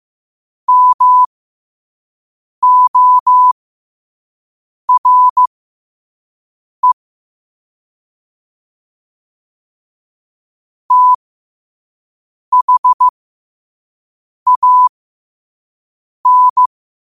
Morse code, heard as a keyed tone, about 15 wpm, spaced as for 5 wpm, 1000 Hz. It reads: MORE THAN
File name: qotdfiles / 5wpm